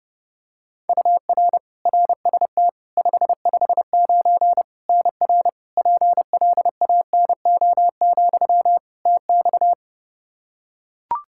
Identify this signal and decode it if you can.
Morse code, heard as a keyed tone, about 32 wpm, 700 Hz. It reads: UR RST 559 NR PLANO, TX
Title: Practice QSO - Exchange 1 - Beginning - 30wpm